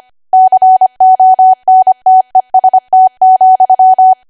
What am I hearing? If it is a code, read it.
Morse code, heard as a keyed tone, about 25 wpm, 750 Hz. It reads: CONTEST,
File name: cwa / W1AW-Level3 / 25WPM